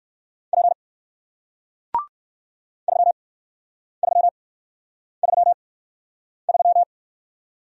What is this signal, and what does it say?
Morse code, heard as a keyed tone, about 58 wpm, spaced as for 9 wpm, 700 Hz. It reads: P 3333